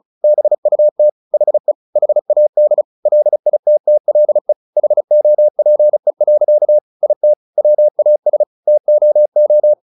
Morse code, heard as a keyed tone, about 36 wpm, 600 Hz. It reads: BUT HE HAD LITTLE HOPE. IT WAS TOO
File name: the old man and the sea 2 35wpm